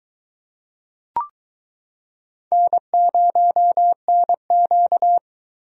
Morse code, heard as a keyed tone, about 22 wpm, 700 Hz. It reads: N0NQ